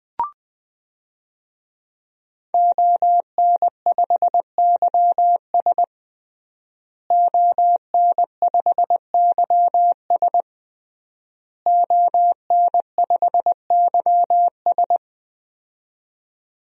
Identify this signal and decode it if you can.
Morse code, heard as a keyed tone, about 20 wpm, 700 Hz. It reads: ON5YS ON5YS ON5YS
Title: DXCC-100 Call Signs - 5 Characters - Repeated 3x - 20wpm